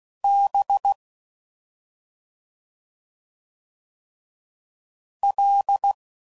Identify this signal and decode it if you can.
Morse code, heard as a keyed tone, about 16 wpm, 800 Hz. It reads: B L
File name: Suffixes / Special ING's / Sngl ING's